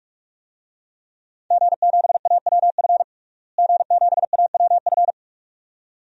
Morse code, heard as a keyed tone, about 44 wpm, 700 Hz. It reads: G7AWF G7AWF
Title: DXCC-100 Call Signs - 5 Characters - Repeated 2x - 45wpm